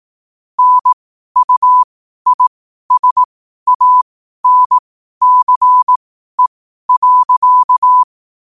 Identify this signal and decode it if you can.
Morse code, heard as a keyed tone, about 18 wpm, spaced as for 12 wpm, 1000 Hz. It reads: NUISANCE.
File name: qotdfiles / 12wpm